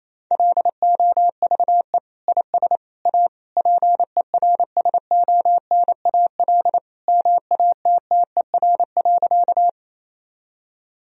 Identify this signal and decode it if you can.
Morse code, heard as a keyed tone, about 28 wpm, 700 Hz. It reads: LOVE IS A PERSONAL MATTER.
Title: Sentences from Top 1000 Words - Review Part 4 - 28wpm